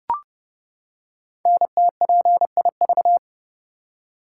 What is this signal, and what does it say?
Morse code, heard as a keyed tone, about 29 wpm, 700 Hz. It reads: NTPIV
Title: Code Groups - 5 Letter:Numbers - 30wpm